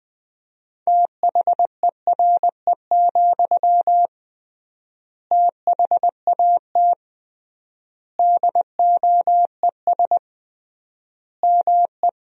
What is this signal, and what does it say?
Morse code, heard as a keyed tone, about 20 wpm, 700 Hz. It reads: THERE, THAT DOES ME
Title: 3x Word-Spacing - Sentences from Top 200 Words - 20wpm